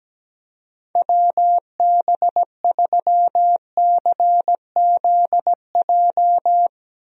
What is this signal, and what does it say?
Morse code, heard as a keyed tone, about 17 wpm, 700 Hz. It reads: WB3CZJ
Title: USA Call Signs - 6 Characters - Repeated 2x - 17wpm